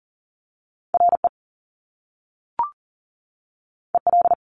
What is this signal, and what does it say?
Morse code, heard as a keyed tone, about 39 wpm, 700 Hz. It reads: RE EL
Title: Top Two Letter Combinations in Words 40wpm